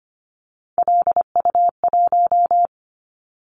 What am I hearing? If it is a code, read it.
Morse code, heard as a keyed tone, about 25 wpm, 700 Hz. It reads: LU1
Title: Callsign Prefixes for Top 50 DXCC - 25wpm